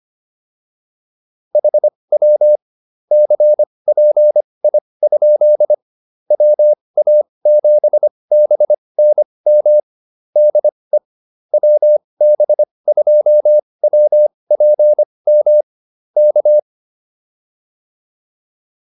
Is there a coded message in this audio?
Yes: Morse code from 1.5 to 16.6 s, heard as a keyed tone, about 25 wpm, spaced as for 20 wpm, 600 Hz. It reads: HW CPI? WA7BNM DE WB2WPM K